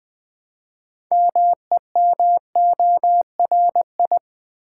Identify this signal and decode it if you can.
Morse code, heard as a keyed tone, about 20 wpm, 700 Hz. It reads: MEMORI
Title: Build A Word - 2 Letters At A Time - 8-Letter Words - 20wpm